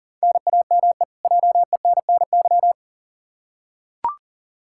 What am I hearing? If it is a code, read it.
Morse code, heard as a keyed tone, about 40 wpm, 700 Hz. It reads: NAME JENNY
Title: K1USN SST QSO Practice 40wpm